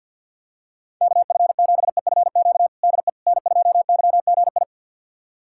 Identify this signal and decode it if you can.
Morse code, heard as a keyed tone, about 52 wpm, 700 Hz. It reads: KF7EFQ DE N2YGI
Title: USA Callsign DE Callsign 50wpm